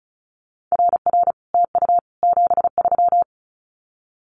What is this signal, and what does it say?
Morse code, heard as a keyed tone, about 34 wpm, 700 Hz. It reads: RR TU 73
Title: Common QSO Phrases Encore 35wpm